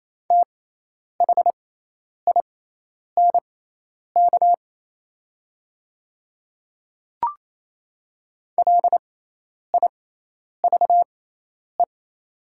Morse code, heard as a keyed tone, about 28 wpm, spaced as for 10 wpm, 700 Hz. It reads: THINK LIVE
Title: 6x Character Spacing - Top 300 Words - 28wpm